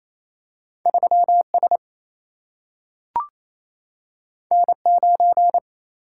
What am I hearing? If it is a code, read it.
Morse code, heard as a keyed tone, about 28 wpm, 700 Hz. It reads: 3S N9
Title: Single-Letter Single-Number 28wpm